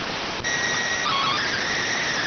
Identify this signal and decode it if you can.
SSTV (slow-scan television), PD 290.